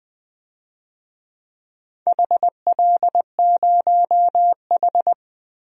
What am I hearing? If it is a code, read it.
Morse code, heard as a keyed tone, about 20 wpm, 700 Hz. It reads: HL0H
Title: DXCC-100 Call Signs - 4 Characters - 20wpm